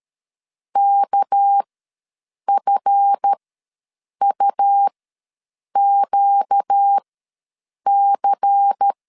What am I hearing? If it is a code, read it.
Morse code, heard as a keyed tone, about 13 wpm, spaced as for 7 wpm, 800 Hz. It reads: KFUQC
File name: LEZIONI / 7 GF9Z / 7 GF ws 12wpm 3140